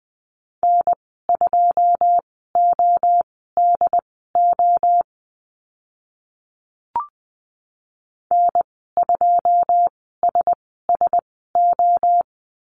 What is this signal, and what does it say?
Morse code, heard as a keyed tone, about 20 wpm, spaced as for 14 wpm, 700 Hz. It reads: N2ODO N2SSO
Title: 082 Lesson 14g - Number 2 - Call Signs 20wpm